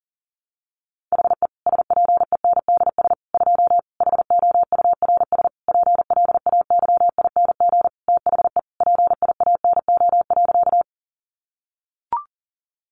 Morse code, heard as a keyed tone, about 40 wpm, 700 Hz. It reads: HE SPENDS 2 HOURS PLAYING THE PIANO.
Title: Additional Easy Phrases 40wpm